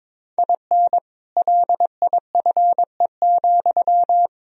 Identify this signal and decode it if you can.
Morse code, heard as a keyed tone, about 22 wpm, 700 Hz. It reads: IN LIFE,
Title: Sentences from Top 1000 Words - Review Part 4 - 22wpm